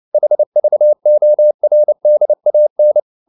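Morse code, heard as a keyed tone, about 29 wpm, 600 Hz. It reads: HVORDAN
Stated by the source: Fast (QRQ) Morse at 29 wpm, from CW med Gnister.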